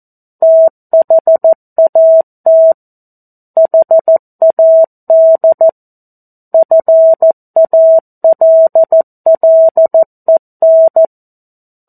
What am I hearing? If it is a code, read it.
Morse code, heard as a keyed tone, about 14 wpm, 650 Hz. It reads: THAT HAD FALLEN
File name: 02-Chapter-02 14 WPM